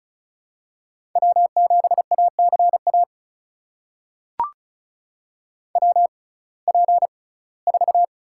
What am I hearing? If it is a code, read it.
Morse code, heard as a keyed tone, about 34 wpm, spaced as for 21 wpm, 700 Hz. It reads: W7ACA WP4
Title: USA Call Signs - 5 Characters - 6x Character Spacing - 35wpm